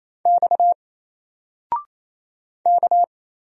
Morse code, heard as a keyed tone, about 28 wpm, 700 Hz. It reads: X K
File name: Single-Letter-Number-28wpm